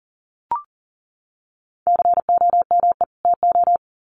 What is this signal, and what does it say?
Morse code, heard as a keyed tone, about 40 wpm, 700 Hz. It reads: COME TO
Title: Sets of 2 Words 40wpm